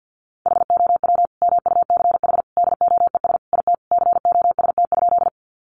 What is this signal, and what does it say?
Morse code, heard as a keyed tone, about 50 wpm, 700 Hz. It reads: HOW MUCH DOES IT COST?
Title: Sentences from Top 800 Words 50wpm